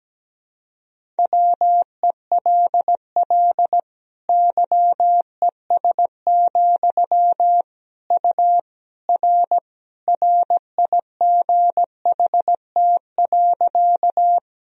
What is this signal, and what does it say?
Morse code, heard as a keyed tone, about 17 wpm, 700 Hz. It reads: WELL YES, U R RIGHT.
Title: Sentences from Top 500 Words and Abbreviations 17wpm